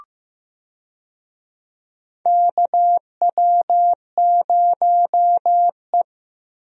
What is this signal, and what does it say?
Morse code, heard as a keyed tone, about 15 wpm, 700 Hz. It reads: KW0E
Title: USA Call Signs - 4 Characters - Repeated 3x - 15wpm